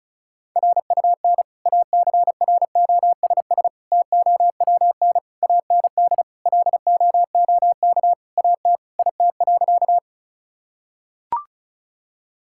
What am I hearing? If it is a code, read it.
Morse code, heard as a keyed tone, about 36 wpm, 690 Hz. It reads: RUN ACROSS TOWN AND LOOK AT IT.
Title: Sentences from Top 1000 Words - Review Part 5 - 35wpm